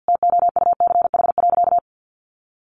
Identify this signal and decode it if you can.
Morse code, heard as a keyed tone, about 50 wpm, 700 Hz. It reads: TOUCH.
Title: Sentences from Top 1500 Words 50wpm